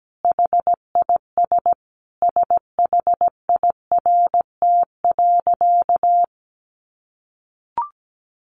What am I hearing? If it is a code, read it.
Morse code, heard as a keyed tone, about 17 wpm, 700 Hz. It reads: HIS SHIRT.
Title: Additional Easy Phrases 17wpm